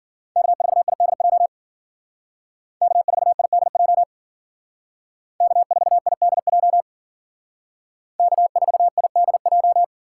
Morse code, heard as a keyed tone, about 52 wpm, 700 Hz. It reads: K4IDJ K4IDJ K4IDJ K4IDJ